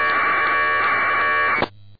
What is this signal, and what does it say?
Packet radio (AX.25, as in APRS).
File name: uploader